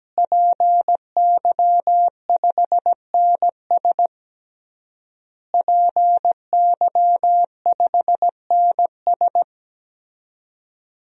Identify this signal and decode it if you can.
Morse code, heard as a keyed tone, about 17 wpm, 700 Hz. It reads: PY5NS PY5NS
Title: DXCC-100 Call Signs - 5 Characters - Repeated 3x - 17wpm